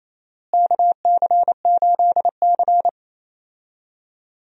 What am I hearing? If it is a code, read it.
Morse code, heard as a keyed tone, about 27 wpm, 700 Hz. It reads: KC8C
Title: USA Call Signs - 4 Characters - Repeated 3x - 28wpm